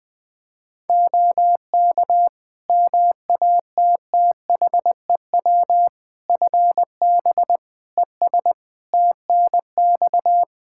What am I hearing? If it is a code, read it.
Morse code, heard as a keyed tone, about 20 wpm, 700 Hz. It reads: OK MATTHEW FB ES TNX
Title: Practice QSO - Exchange 2 - Info - 20wpm